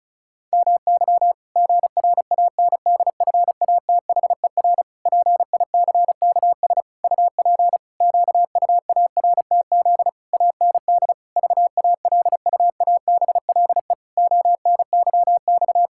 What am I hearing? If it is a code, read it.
Morse code, heard as a keyed tone, about 36 wpm, 700 Hz. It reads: MY GRANDFATHER PICKS UP QUARTZ AND VALUABLE ONYX
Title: Pangram Challenge v2 - 35wpm